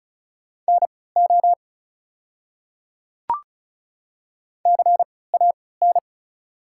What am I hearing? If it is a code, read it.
Morse code, heard as a keyed tone, about 36 wpm, spaced as for 20 wpm, 690 Hz. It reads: NO CAN
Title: 3x Character Spacing - Top 100 Words - 35wpm